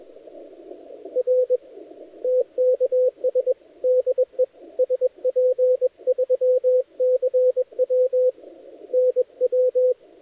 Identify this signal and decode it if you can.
Morse code, heard as a keyed tone, about 23 wpm, 510 Hz, with radio noise. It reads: R TKS DE SP3CW NW
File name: dlazaawansowanych / nagraniazpasma / titawaNETcontrol